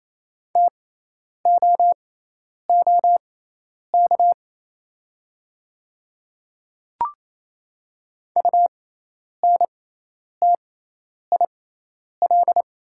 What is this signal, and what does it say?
Morse code, heard as a keyed tone, about 28 wpm, spaced as for 10 wpm, 700 Hz. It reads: TOOK UNTIL
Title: 6x Character Spacing - Top 500 Words - 28wpm